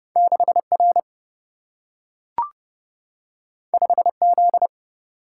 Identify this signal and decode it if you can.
Morse code, heard as a keyed tone, about 30 wpm, 700 Hz. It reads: 6 R 5 Z
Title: Single-Letter Single-Number 30wpm